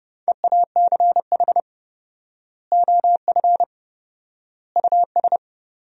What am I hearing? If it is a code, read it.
Morse code, heard as a keyed tone, about 30 wpm, 700 Hz. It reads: EACH OF US
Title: 4x Word-Spacing - Sentences from Top 500 Words - 30wpm